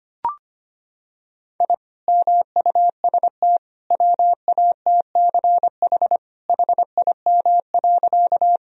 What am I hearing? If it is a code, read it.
Morse code, heard as a keyed tone, about 25 wpm, 700 Hz. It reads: I MUST WATCH HIM.
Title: Sentences from Top 1000 Words - Review Part 1 - 25wpm